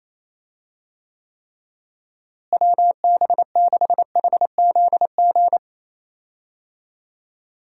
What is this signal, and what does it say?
Morse code, heard as a keyed tone, about 28 wpm, 700 Hz. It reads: WB6HZG
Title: USA Call Signs - 6 Characters - Repeated 3x - 28wpm